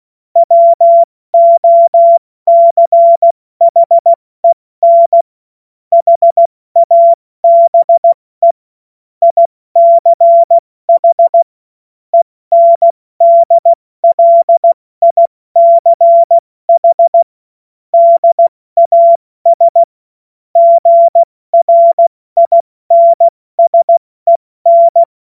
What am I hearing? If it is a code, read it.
Morse code, heard as a keyed tone, about 16 wpm, spaced as for 14 wpm, 680 Hz. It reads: WOCHEN HABE ICH ENDLICH DAS GRINSEN
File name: Morsekurs / Witze / Witz 02 April 2024 70 F80